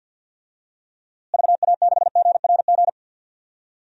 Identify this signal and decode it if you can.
Morse code, heard as a keyed tone, about 47 wpm, 700 Hz. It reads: VA6GRG
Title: DXCC-100 Call Signs - 6 Characters - Repeated 3x - 50wpm